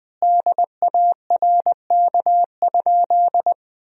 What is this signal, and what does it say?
Morse code, heard as a keyed tone, about 21 wpm, 700 Hz. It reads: DARK?
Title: 4x Word-Spacing - Sentences from Top 500 Words - 20wpm